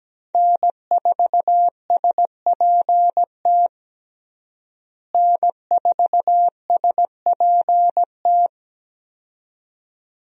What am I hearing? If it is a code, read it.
Morse code, heard as a keyed tone, about 17 wpm, 700 Hz. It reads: N4SPT N4SPT
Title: USA Call Signs - 5 Characters - Repeated 2x - 17wpm